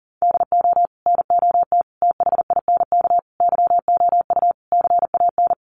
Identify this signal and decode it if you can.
Morse code, heard as a keyed tone, about 40 wpm, 700 Hz. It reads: DO NOT THINK YOU CAN
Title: Sets of 5 Words 40wpm